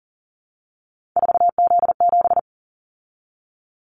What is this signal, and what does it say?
Morse code, heard as a keyed tone, about 38 wpm, 700 Hz. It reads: 4Z7
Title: Callsign Prefixes for Top 50 DXCC - 40wpm